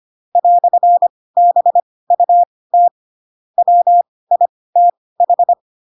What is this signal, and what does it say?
Morse code, heard as a keyed tone, about 25 wpm, spaced as for 18 wpm, 700 Hz. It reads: "BUT WITH
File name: Pinocchio ch3 12 25 ik1hge